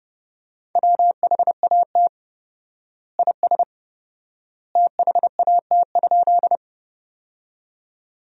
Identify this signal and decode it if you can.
Morse code, heard as a keyed tone, about 32 wpm, 700 Hz. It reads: WHAT IS THAT?
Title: 4x Word-Spacing - Sentences from Top 100 Words - 30wpm